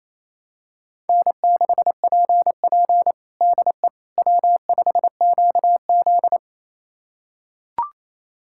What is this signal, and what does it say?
Morse code, heard as a keyed tone, about 28 wpm, 700 Hz. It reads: N6PP DE W5QZ
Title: USA Callsign DE Callsign - 4 Characters - 28wpm